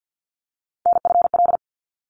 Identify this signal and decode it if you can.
Morse code, heard as a keyed tone, about 50 wpm, 700 Hz. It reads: NFL
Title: ARRL Sections 50wpm